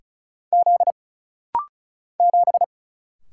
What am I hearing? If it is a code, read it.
Morse code, heard as a keyed tone, about 36 wpm, 700 Hz. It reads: Z 7
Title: Single Letter-Number Alt - Rapid-Fire 35wpm